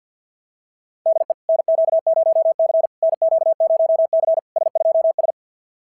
Morse code, heard as a keyed tone, about 46 wpm, 650 Hz. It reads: DE NQ0C NQ0C S2S
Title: SOTA QSO Practice 50wpm